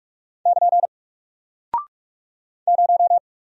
Morse code, heard as a keyed tone, about 44 wpm, 700 Hz. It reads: <KN> 0
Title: Single Character (Letter:Number:Punctuation:Prosign) Alt 45wpm